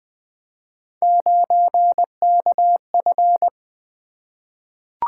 Morse code, heard as a keyed tone, about 20 wpm, 700 Hz. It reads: 9KF
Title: Three Letter-Number Combinations - 20wpm